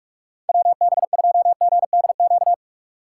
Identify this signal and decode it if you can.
Morse code, heard as a keyed tone, about 46 wpm, 700 Hz. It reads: WB2GDQ
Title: USA Call Signs - 6 Characters - 6x Character Spacing - 45wpm